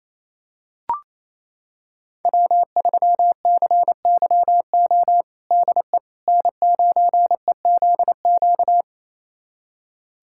Morse code, heard as a keyed tone, about 28 wpm, 690 Hz. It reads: W3CYO DE N9EZQ
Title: USA Callsign DE Callsign - 5 Characters - 28wpm